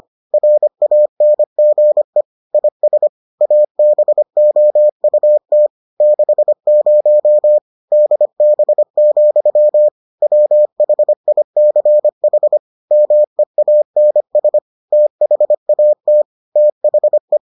Morse code, heard as a keyed tone, about 25 wpm, 600 Hz. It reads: RANGE IS ABOUT 60 DB, WHICH MEANS THAT THE